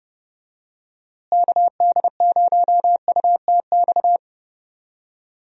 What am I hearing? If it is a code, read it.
Morse code, heard as a keyed tone, about 30 wpm, 700 Hz. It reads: KD0UTX
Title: USA Call Signs - 6 Characters - Repeated 2x - 30wpm